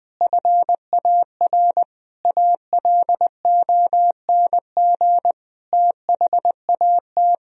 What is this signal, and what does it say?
Morse code, heard as a keyed tone, about 20 wpm, 700 Hz. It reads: FAR ALONG THAT
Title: Sentences from Top 1000 Words - Review Part 2 - 20wpm